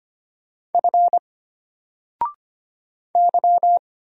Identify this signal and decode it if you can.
Morse code, heard as a keyed tone, about 25 wpm, 700 Hz. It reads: F Y
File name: Single-Letter-Number-25wpm